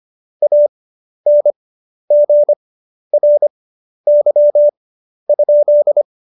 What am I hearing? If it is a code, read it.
Morse code, heard as a keyed tone, about 25 wpm, spaced as for 11 wpm, 590 Hz. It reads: ANGRY?
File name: Jokes-25-char-10-eff